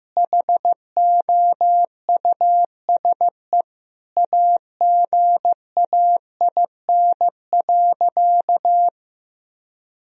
Morse code, heard as a keyed tone, about 15 wpm, 700 Hz. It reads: HOUSE AGAIN.
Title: Sentences from Top 1000 Words - Review Part 3 - 15wpm